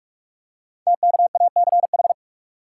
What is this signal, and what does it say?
Morse code, heard as a keyed tone, about 44 wpm, 700 Hz. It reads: TKACH
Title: Code Groups - 5 Letters - 45wpm